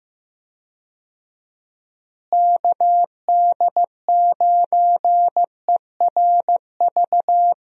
Morse code, heard as a keyed tone, about 15 wpm, 700 Hz. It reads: KD9ERV
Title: USA Call Signs - 6 Characters - Repeated 3x - 15wpm